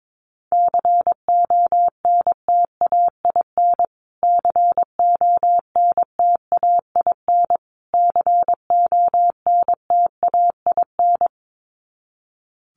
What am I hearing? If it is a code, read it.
Morse code, heard as a keyed tone, about 22 wpm, 700 Hz. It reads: CONTAIN CONTAIN CONTAIN
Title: Top 500 Words Repeated 22wpm